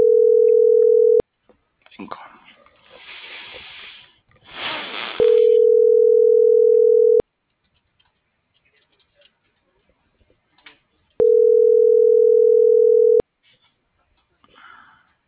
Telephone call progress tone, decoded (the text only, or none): ringback tone